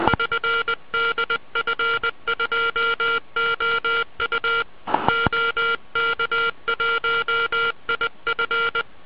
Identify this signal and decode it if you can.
Morse code, heard as a keyed tone, about 20 wpm, 1390 Hz. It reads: <SN> DF2OU OK1IF